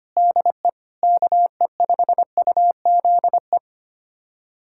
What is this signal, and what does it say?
Morse code, heard as a keyed tone, about 25 wpm, 700 Hz. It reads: DE KE5UZE
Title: USA Callsign DE Callsign - 6 Characters - 25wpm